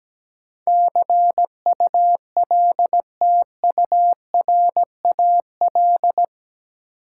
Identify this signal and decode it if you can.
Morse code, heard as a keyed tone, about 17 wpm, 700 Hz. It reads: CULTURAL
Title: Sentences from Top 1500 Words - New Word Drill - 17wpm